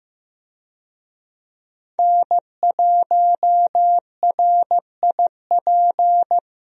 Morse code, heard as a keyed tone, about 15 wpm, 700 Hz. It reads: N1RIP
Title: USA Call Signs - 5 Characters - Repeated 3x - 15wpm